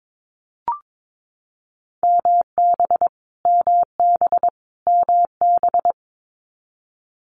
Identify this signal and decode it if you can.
Morse code, heard as a keyed tone, about 22 wpm, 700 Hz. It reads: MB MB MB
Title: Multi-letter Phonemes Repeated 3x 22wpm